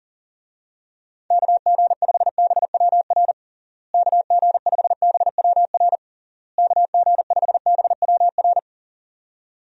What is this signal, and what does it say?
Morse code, heard as a keyed tone, about 40 wpm, 700 Hz. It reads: KG5BWR KG5BWR KG5BWR
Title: USA Call Signs - 6 Characters - Repeated 3x - 40wpm